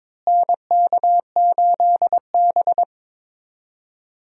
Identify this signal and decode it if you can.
Morse code, heard as a keyed tone, about 22 wpm, 700 Hz. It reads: NK8B
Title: USA Call Signs - 4 Characters - Repeated 2x - 22wpm